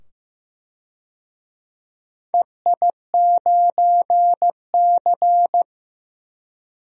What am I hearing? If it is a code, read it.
Morse code, heard as a keyed tone, about 15 wpm, 700 Hz. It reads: EI9C
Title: DXCC-100 Call Signs - 4 Characters - 15wpm